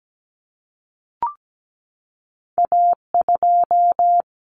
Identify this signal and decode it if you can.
Morse code, heard as a keyed tone, about 17 wpm, 700 Hz. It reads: A2
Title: Common Cut Numbers - 2-Digit 17wpm